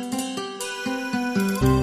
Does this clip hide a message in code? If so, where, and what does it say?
No.